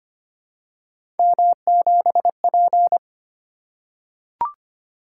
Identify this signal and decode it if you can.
Morse code, heard as a keyed tone, about 25 wpm, 700 Hz. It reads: M7P
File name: Three Letter-Number Combinations - 25wpm